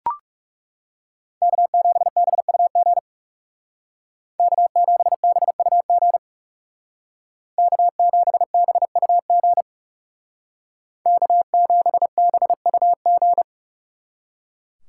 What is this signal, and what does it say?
Morse code, heard as a keyed tone, about 39 wpm, 700 Hz. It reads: K7BUG K7BUG K7BUG K7BUG